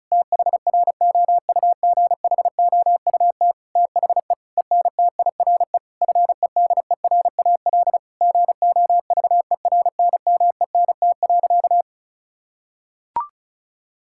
Morse code, heard as a keyed tone, about 36 wpm, 700 Hz. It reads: THROUGHOUT THE ENTIRE FEDERAL GOVERNMENT.
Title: Sentences from Top 1000 Words - Review Part 5 - 35wpm